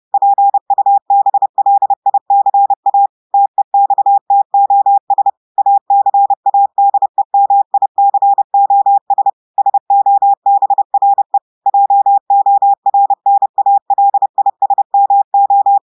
Morse code, heard as a keyed tone, about 30 wpm, 800 Hz. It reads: PUBLICA TEXTOS ACADEMICOS SOBRE JORNALISMO
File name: audios / portugais / portugais texte en clair 1 vitesse 30 mots minutes